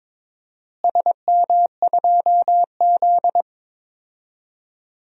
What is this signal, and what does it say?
Morse code, heard as a keyed tone, about 22 wpm, 700 Hz. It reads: SM2Z
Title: DXCC-100 Call Signs - 4 Characters - Repeated 3x - 22wpm